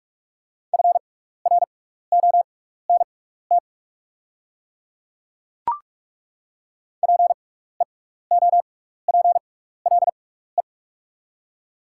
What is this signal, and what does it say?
Morse code, heard as a keyed tone, about 42 wpm, spaced as for 15 wpm, 700 Hz. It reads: FRONT PEOPLE
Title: 6x Character Spacing - Top 500 Words - 45wpm